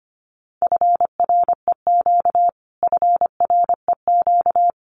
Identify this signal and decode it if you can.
Morse code, heard as a keyed tone, about 25 wpm, 700 Hz. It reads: FREQ FREQ
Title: Abbreviations Repeated 3x - 25wpm